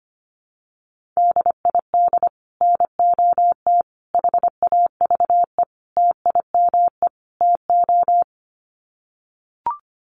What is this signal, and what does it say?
Morse code, heard as a keyed tone, about 25 wpm, 700 Hz. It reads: DID NOT HAVE TIME TO
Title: Sets of 5 Words Encore 25wpm